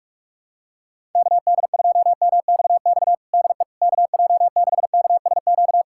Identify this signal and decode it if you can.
Morse code, heard as a keyed tone, about 46 wpm, 700 Hz. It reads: KD2MXX DE KJ6KSQ
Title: USA Callsign DE Callsign 45wpm